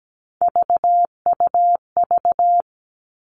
Morse code, heard as a keyed tone, about 17 wpm, 700 Hz. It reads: VUV
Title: Three Challenging Letter-Number Combinations 17wpm